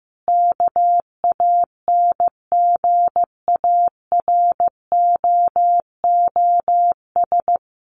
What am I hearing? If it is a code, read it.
Morse code, heard as a keyed tone, about 15 wpm, 700 Hz. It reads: KANGAROOS
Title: Easy Phrases 15wpm